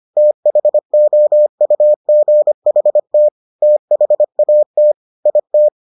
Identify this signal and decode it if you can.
Morse code, heard as a keyed tone, about 25 wpm, 600 Hz. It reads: THOUGHT THAT IT